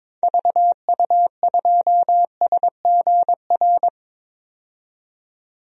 Morse code, heard as a keyed tone, about 22 wpm, 700 Hz. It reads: VU2SGR
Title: DXCC-100 Call Signs - 6 Characters - 22wpm